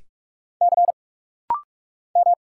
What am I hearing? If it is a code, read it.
Morse code, heard as a keyed tone, about 45 wpm, 700 Hz. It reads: C M
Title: Single Letter-Number Alt - Rapid-Fire 45wpm